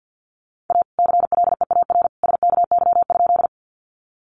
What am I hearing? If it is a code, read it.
Morse code, heard as a keyed tone, about 50 wpm, 700 Hz. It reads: A CLEAR SKY?
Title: Sentences from Top 1200 Words 50wpm